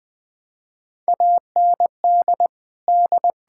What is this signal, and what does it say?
Morse code, heard as a keyed tone, about 20 wpm, 700 Hz. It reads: AND D
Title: Sentences with 7-Word Phrases - 20wpm